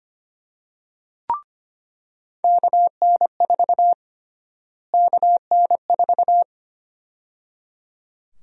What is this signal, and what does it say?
Morse code, heard as a keyed tone, about 25 wpm, 700 Hz. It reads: KN4 KN4
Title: USA Call Sign Prefixes - Repeated 2x - 25wpm